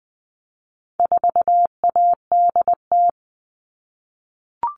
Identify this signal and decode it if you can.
Morse code, heard as a keyed tone, about 20 wpm, 700 Hz. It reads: 4ADT
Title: All Cut Numbers - 4-Digit 20wpm